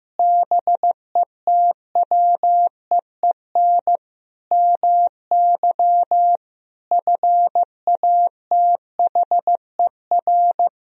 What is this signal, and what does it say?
Morse code, heard as a keyed tone, about 15 wpm, 700 Hz. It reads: BETWEEN MY FATHER
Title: Sentences from Top 1000 Words - Review Part 3 - 15wpm